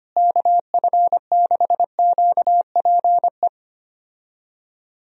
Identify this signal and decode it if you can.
Morse code, heard as a keyed tone, about 25 wpm, 700 Hz. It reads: KF6QPE
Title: USA Call Signs - 6 Characters - Repeated 3x - 25wpm